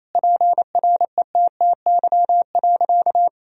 Morse code, heard as a keyed tone, about 28 wpm, 700 Hz. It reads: PRETTY.
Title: Sentences from Top 1000 Words - Review Part 1 - 28wpm